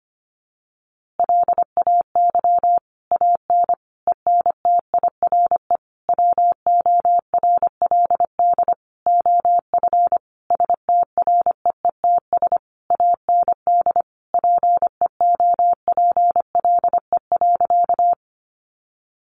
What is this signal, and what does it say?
Morse code, heard as a keyed tone, about 25 wpm, 700 Hz. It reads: LAY AN ENTIRE WORLD OF STREETS AND PEOPLE.
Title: A Sound Of Thunder 25wpm